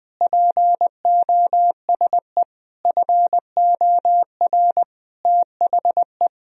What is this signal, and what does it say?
Morse code, heard as a keyed tone, about 20 wpm, 700 Hz. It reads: POSE FOR THE
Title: Sentences from Top 1000 Words - Review Part 2 - 20wpm